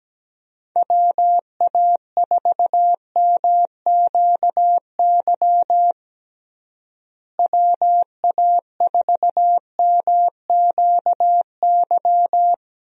Morse code, heard as a keyed tone, about 17 wpm, 700 Hz. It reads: WA4MQY WA4MQY
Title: USA Call Signs - 6 Characters - Repeated 3x - 17wpm